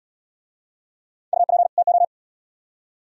Morse code, heard as a keyed tone, about 81 wpm, 700 Hz. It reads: UP TO